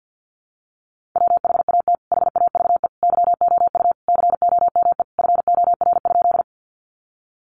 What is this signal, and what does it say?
Morse code, heard as a keyed tone, about 50 wpm, 700 Hz. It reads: WHAT HAVE YOU COME FOR?
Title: Sentences from Top 100 Words 50wpm